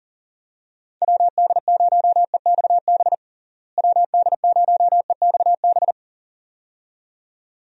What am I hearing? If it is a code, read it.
Morse code, heard as a keyed tone, about 40 wpm, 700 Hz. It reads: WD0EXB WD0EXB
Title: USA Call Signs - 6 Characters - Repeated 2x - 40wpm